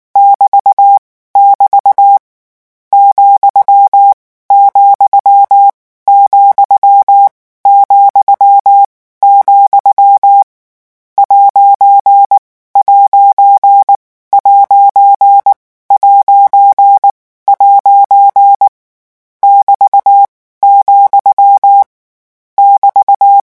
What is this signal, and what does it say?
Morse code, heard as a keyed tone, about 19 wpm, spaced as for 14 wpm, 800 Hz. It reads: <BT> <BT> , , ,,, ''''' <BT>, <BT>
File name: morse mp3 / lecon14-vitesse 18